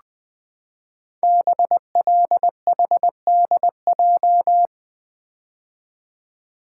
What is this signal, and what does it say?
Morse code, heard as a keyed tone, about 20 wpm, 700 Hz. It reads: BLHDJ
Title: Code Groups - 5 Letters - 20wpm